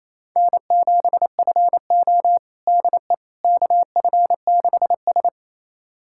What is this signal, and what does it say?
Morse code, heard as a keyed tone, about 28 wpm, 700 Hz. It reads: N7FO DE KF6S
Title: USA Callsign DE Callsign - 4 Characters - 28wpm